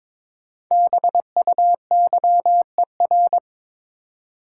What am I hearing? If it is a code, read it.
Morse code, heard as a keyed tone, about 22 wpm, 700 Hz. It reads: BUYER
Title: Five Letter Words - 22wpm